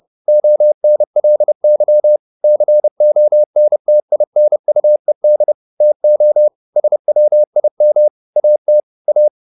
Morse code, heard as a keyed tone, about 30 wpm, 600 Hz. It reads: ONLY CONTINUED TO SWIM AT A
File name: the old man and the sea / the old man and the sea 1 30wpm